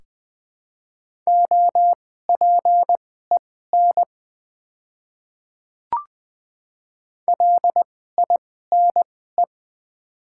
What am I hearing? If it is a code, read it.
Morse code, heard as a keyed tone, about 20 wpm, spaced as for 14 wpm, 690 Hz. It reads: OPEN LINE
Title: 2x Character Spacing - Top 300 Words - 20wpm